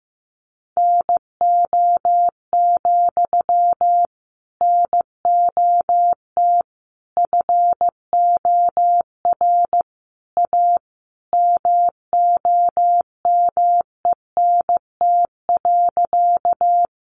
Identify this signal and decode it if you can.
Morse code, heard as a keyed tone, about 15 wpm, 700 Hz. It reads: NO, NOT FOR A MOMENT.
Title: Sentences from Top 1000 Words 15wpm